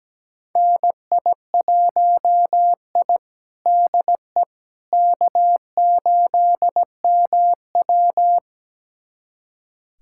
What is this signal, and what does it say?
Morse code, heard as a keyed tone, about 17 wpm, 700 Hz. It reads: NI1I DE K8MW
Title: USA Callsign DE Callsign - 4 Characters - 17wpm